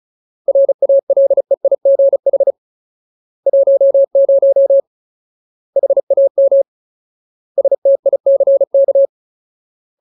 Morse code, heard as a keyed tone, about 35 wpm, spaced as for 28 wpm, 550 Hz. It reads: RALEIGH 10 HAMSTICK
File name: Ultimate Word List / 35WPM Ultimate